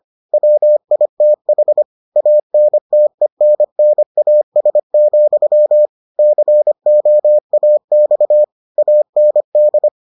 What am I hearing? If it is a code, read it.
Morse code, heard as a keyed tone, about 25 wpm, 600 Hz. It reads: WITH ANTENNAS, COAX AND